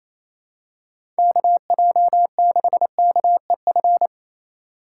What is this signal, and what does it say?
Morse code, heard as a keyed tone, about 27 wpm, 700 Hz. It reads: KJ6KEF
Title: USA Call Signs - 6 Characters - 8x Character Spacing - 28wpm